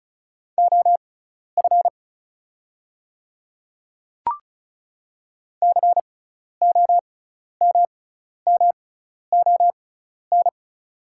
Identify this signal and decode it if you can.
Morse code, heard as a keyed tone, about 36 wpm, spaced as for 12 wpm, 700 Hz. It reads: OF COMMON
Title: 6x Character Spacing - Top 500 Words - 35wpm